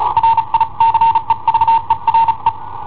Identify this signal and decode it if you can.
Morse code, heard as a keyed tone, about 35 wpm, spaced as for 31 wpm, 910 Hz. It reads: RICEVERE
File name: i7ohp / cassetta i7ohp B